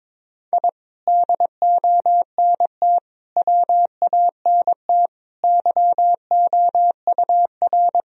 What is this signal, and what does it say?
Morse code, heard as a keyed tone, about 22 wpm, 700 Hz. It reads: I DONT WANT YOUR
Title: Sentences from Top 1000 Words - Review Part 3 - 22wpm